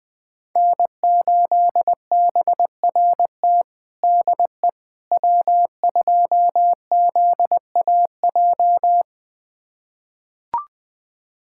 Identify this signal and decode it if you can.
Morse code, heard as a keyed tone, about 21 wpm, 700 Hz. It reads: N8BRT DE W2ZAJ
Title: USA Callsign DE Callsign - 5 Characters - 20wpm